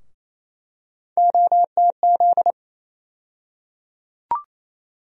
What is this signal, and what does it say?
Morse code, heard as a keyed tone, about 28 wpm, 700 Hz. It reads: OTZ
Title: USA Call Sign Postfixes - 3 Letters 28wpm